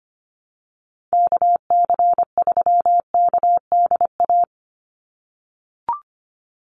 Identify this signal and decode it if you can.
Morse code, heard as a keyed tone, about 25 wpm, 700 Hz. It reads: KC3KDA
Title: USA Call Signs 25wpm